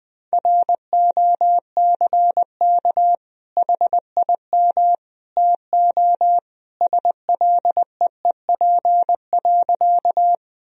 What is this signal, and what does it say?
Morse code, heard as a keyed tone, about 20 wpm, 700 Hz. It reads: ROCK HIM TO SLEEP.
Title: Sentences from Top 1000 Words - Review Part 2 - 20wpm